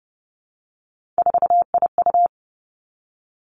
Morse code, heard as a keyed tone, about 30 wpm, 700 Hz. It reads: 4IU
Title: Three Challenging Letter-Number Combinations 30wpm